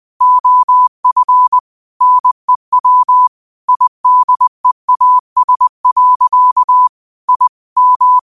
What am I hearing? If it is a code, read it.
Morse code, heard as a keyed tone, about 20 wpm, 1000 Hz. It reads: OF NEW IDEAS. I M
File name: qotdfiles / 20wpm